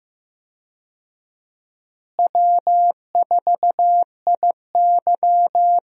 Morse code, heard as a keyed tone, about 15 wpm, 700 Hz. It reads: W4IY
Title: USA Call Signs - 4 Characters - Repeated 3x - 15wpm